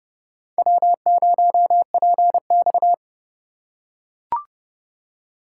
Morse code, heard as a keyed tone, about 26 wpm, 700 Hz. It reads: W0PX